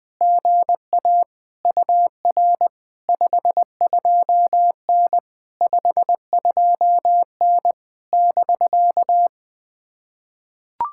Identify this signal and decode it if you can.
Morse code, heard as a keyed tone, about 19 wpm, 700 Hz. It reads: GA UR 52N 52N <BK>
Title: POTA QSO Practice - 20wpm